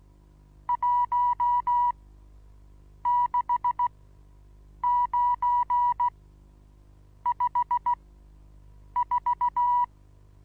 Morse code, heard as a keyed tone, about 16 wpm, spaced as for 7 wpm, 970 Hz. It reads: 16954
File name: Onlinelehrgang / morsen / DJ4UF Morsen25